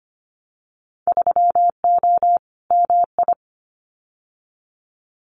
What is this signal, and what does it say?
Morse code, heard as a keyed tone, about 25 wpm, 700 Hz. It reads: 3O MI
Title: Winter Field Day Exchanges 25wpm